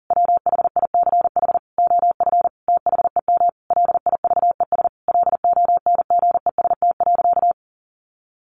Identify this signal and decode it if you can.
Morse code, heard as a keyed tone, about 40 wpm, 700 Hz. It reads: WHICH OF THEM LIVES LONGEST.
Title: A Sound Of Thunder 40wpm